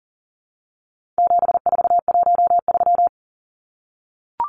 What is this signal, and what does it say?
Morse code, heard as a keyed tone, about 39 wpm, 700 Hz. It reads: 7413
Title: Numbers - 4-Digit 40wpm